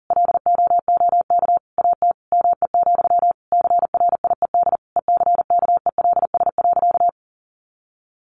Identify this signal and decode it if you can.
Morse code, heard as a keyed tone, about 40 wpm, 700 Hz. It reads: LOOK AT ME, CRIED ECKELS.
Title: A Sound Of Thunder 40wpm